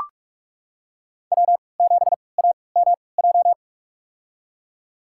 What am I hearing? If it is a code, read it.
Morse code, heard as a keyed tone, about 43 wpm, spaced as for 25 wpm, 700 Hz. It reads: W7AMJ
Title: USA Call Signs - 5 Characters - 3x Character Spacing - 45wpm